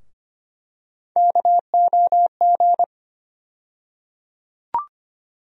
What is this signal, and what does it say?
Morse code, heard as a keyed tone, about 25 wpm, 700 Hz. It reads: KOG
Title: USA Call Sign Postfixes - 3 Letters 25wpm